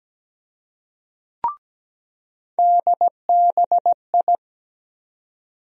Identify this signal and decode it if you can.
Morse code, heard as a keyed tone, about 17 wpm, 700 Hz. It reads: DBI
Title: USA Call Sign Postfixes - 3 Letters 17wpm